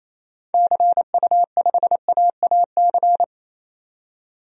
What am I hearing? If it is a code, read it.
Morse code, heard as a keyed tone, about 28 wpm, 700 Hz. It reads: CU5AAC
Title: DXCC-100 Call Signs - 6 Characters - 28wpm